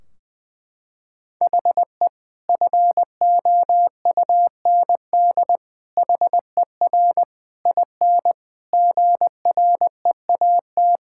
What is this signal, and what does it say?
Morse code, heard as a keyed tone, about 20 wpm, 700 Hz. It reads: HE FOUND HER IN GREAT
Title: Sentences from Top 1600 Words 20wpm